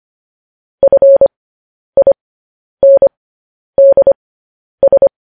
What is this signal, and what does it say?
Morse code, heard as a keyed tone, about 25 wpm, spaced as for 10 wpm, 550 Hz. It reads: FINDS
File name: podcast / News headlines-1x-25x10